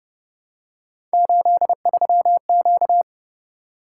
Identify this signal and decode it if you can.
Morse code, heard as a keyed tone, about 31 wpm, 700 Hz. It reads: 83Q